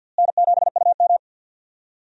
Morse code, heard as a keyed tone, about 46 wpm, 700 Hz. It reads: N7UM
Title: USA Call Signs - 4 Characters - Repeated 3x - 50wpm